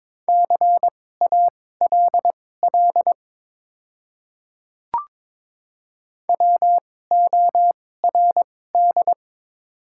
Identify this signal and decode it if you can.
Morse code, heard as a keyed tone, about 23 wpm, spaced as for 16 wpm, 700 Hz. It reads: CALL WORD
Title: 2x Character Spacing - Top 100 Words - 22wpm